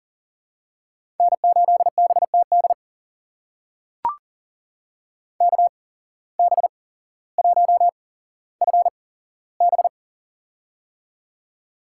Morse code, heard as a keyed tone, about 40 wpm, spaced as for 17 wpm, 700 Hz. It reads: N8BTD KB1FB
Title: USA Call Signs - 5 Characters - 8x Character Spacing - 40wpm